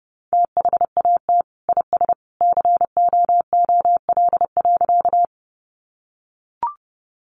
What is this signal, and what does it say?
Morse code, heard as a keyed tone, about 30 wpm, 700 Hz. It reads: THAT IS COOL.
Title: Sentences from Top 1200 Words 30wpm